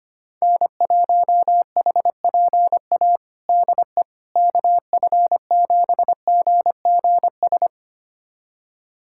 Morse code, heard as a keyed tone, about 25 wpm, 700 Hz. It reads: N1HPA DE KF7GGS
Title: USA Callsign DE Callsign 25wpm